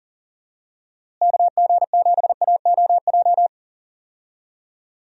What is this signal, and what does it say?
Morse code, heard as a keyed tone, about 40 wpm, 700 Hz. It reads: KG7AOJ
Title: USA Call Signs - 6 Characters - Repeated 3x - 40wpm